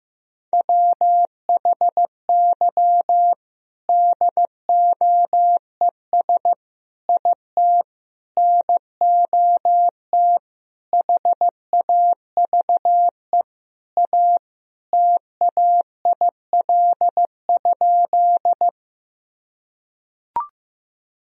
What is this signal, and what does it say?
Morse code, heard as a keyed tone, about 15 wpm, 700 Hz. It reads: WHY DOES IT NOT HAVE A TAIL?
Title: Sentences from Top 1000 Words - Review Part 3 - 15wpm